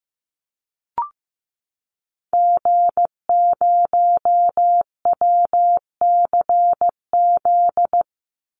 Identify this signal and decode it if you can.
Morse code, heard as a keyed tone, about 15 wpm, 700 Hz. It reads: G0WCZ
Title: Call Signs Repeated 3x - 15wpm